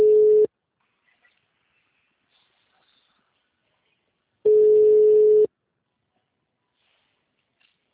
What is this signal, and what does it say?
Ringback tone (Europe, 425 Hz).
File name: out- -2008- - - .13849